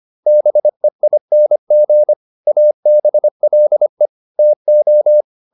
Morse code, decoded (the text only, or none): BEING ABLE TO